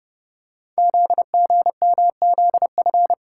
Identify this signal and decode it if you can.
Morse code, heard as a keyed tone, about 28 wpm, 700 Hz. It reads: ZGMZF
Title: Code Groups - 5 Letters - 30wpm